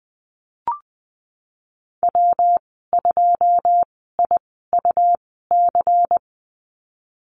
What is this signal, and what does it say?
Morse code, heard as a keyed tone, about 19 wpm, spaced as for 14 wpm, 700 Hz. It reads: W2IUC
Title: 154 Lesson 25g - Letter p - Call Signs 20wpm